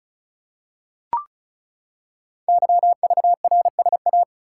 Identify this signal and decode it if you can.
Morse code, heard as a keyed tone, about 36 wpm, 700 Hz. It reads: YVRSA
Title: Code Groups - 5 Letters - 35wpm